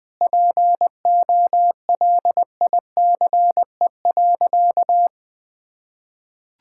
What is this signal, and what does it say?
Morse code, heard as a keyed tone, about 20 wpm, 700 Hz. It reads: POLICE.
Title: Sentences with 7-Word Phrases - 20wpm